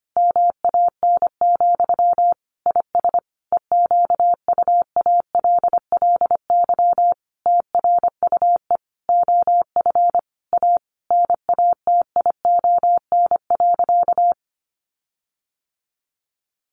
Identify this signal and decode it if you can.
Morse code, heard as a keyed tone, about 25 wpm, 700 Hz. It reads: MAN, IS EQUALLY TRUE OF A NATION.
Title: Sentences from Top 1400 Words 25wpm